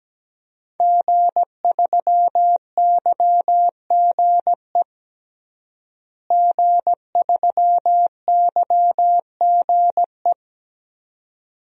Morse code, heard as a keyed tone, about 17 wpm, 700 Hz. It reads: G3YGE G3YGE
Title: DXCC-100 Call Signs - 5 Characters - Repeated 2x - 17wpm